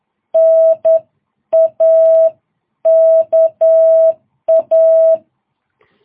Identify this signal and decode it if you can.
Morse code, heard as a keyed tone, about 10 wpm, spaced as for 9 wpm, 660 Hz. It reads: NAKA
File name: jh8cbh / 2022cw